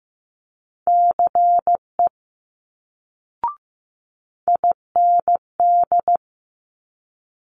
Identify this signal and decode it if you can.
Morse code, heard as a keyed tone, about 15 wpm, 700 Hz. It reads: CE IND
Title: Multi-letter Phonemes Quick Fire 15wpm